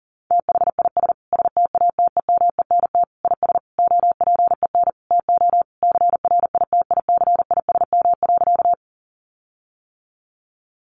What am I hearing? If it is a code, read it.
Morse code, heard as a keyed tone, about 40 wpm, 700 Hz. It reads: THIS STATEMENT IS OPEN TO CRITICISM.
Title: Sentences from Top 1300 Words 40wpm